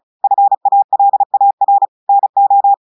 Morse code, heard as a keyed tone, about 33 wpm, 800 Hz. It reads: FALAR NO
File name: audios / portugais / portugais texte en clair 2 vitesse 35 mots minutes